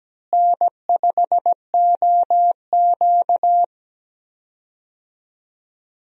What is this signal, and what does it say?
Morse code, heard as a keyed tone, about 17 wpm, 700 Hz. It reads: N5OQ
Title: USA Call Signs - 4 Characters - Repeated 2x - 17wpm